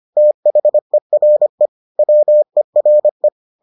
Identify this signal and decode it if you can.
Morse code, heard as a keyed tone, about 25 wpm, 600 Hz. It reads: THERE WERE